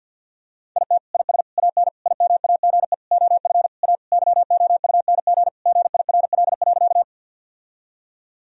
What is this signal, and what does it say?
Morse code, heard as a keyed tone, about 50 wpm, 700 Hz. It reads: IT IS AN IMAGE OF A YOUNG GIRL.
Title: Sentences from Top 1000 Words - Review Part 4 - 50wpm